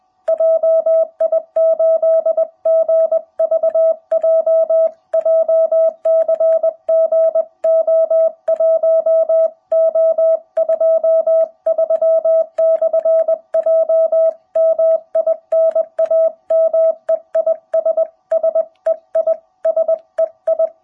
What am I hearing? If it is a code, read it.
Morse code, heard as a keyed tone, about 20 wpm, 660 Hz. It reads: JI8GVJ JCG O1O23/J MI NAME IS SEI SEI
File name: jh8cbh / 2021cw